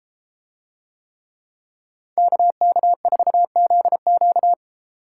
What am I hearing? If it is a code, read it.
Morse code, heard as a keyed tone, about 33 wpm, 700 Hz. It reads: KK4ZQ